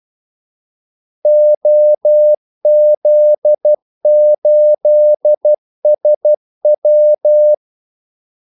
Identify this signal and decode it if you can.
Morse code, heard as a keyed tone, about 12 wpm, 600 Hz. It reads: OZ8SW
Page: 2 ekstra ordmellemrum mellem kaldesignalerne